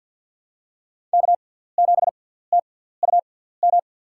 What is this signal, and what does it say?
Morse code, heard as a keyed tone, about 53 wpm, spaced as for 17 wpm, 700 Hz. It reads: K7TUM
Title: USA Call Signs - 5 Characters - 6x Character Spacing - 50wpm